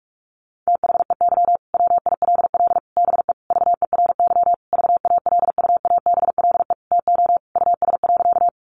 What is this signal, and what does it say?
Morse code, heard as a keyed tone, about 44 wpm, 700 Hz. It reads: THEY WILL BE VERY VALUABLE TO US.
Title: Sentences from Top 1100 Words 45wpm